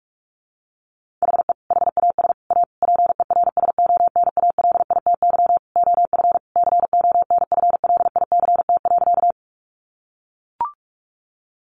Morse code, heard as a keyed tone, about 44 wpm, 700 Hz. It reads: HE HAS A PERSONALITY OF CONFLICT.
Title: Sentences from Top 1400 Words 45wpm